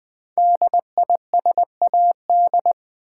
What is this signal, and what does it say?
Morse code, heard as a keyed tone, about 20 wpm, 700 Hz. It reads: DISAD
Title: Build A Word - 5 Letters At A Time - 12-Letter Words - 20wpm